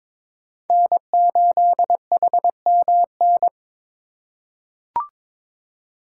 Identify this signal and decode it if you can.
Morse code, heard as a keyed tone, about 22 wpm, 700 Hz. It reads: N8HMN
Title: USA Call Signs - 8x Character Spacing - 22wpm